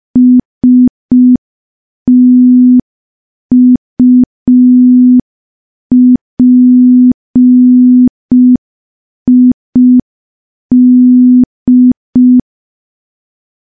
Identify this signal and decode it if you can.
Morse code, heard as a keyed tone, about 5 wpm, 250 Hz. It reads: STUPID
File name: morse / morse-training-05WPM